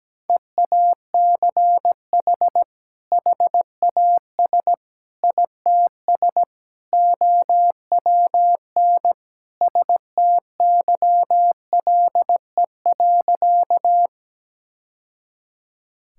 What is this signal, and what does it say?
Morse code, heard as a keyed tone, about 17 wpm, 700 Hz. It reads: EACH HAS ITS OWN STYLE.
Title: Sentences from Top 1000 Words - Review Part 5 - 17wpm